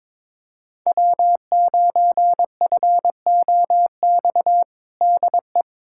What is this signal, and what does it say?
Morse code, heard as a keyed tone, about 22 wpm, 700 Hz. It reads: W9FOX DE
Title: USA Callsign DE Callsign - 5 Characters - 22wpm